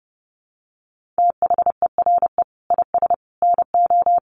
Morse code, heard as a keyed tone, about 30 wpm, 700 Hz. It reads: THERE IS NO
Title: Sentences from Top 1500 Words 30wpm